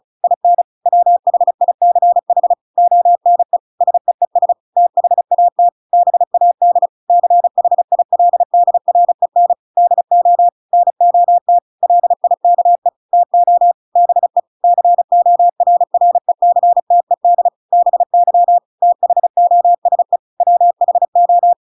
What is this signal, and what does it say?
Morse code, heard as a keyed tone, about 34 wpm, 700 Hz. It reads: IN WHICH ONE SEES THAT BAD CHILDREN DO NOT LIKE TO BE CORRECTED BY THOSE WHO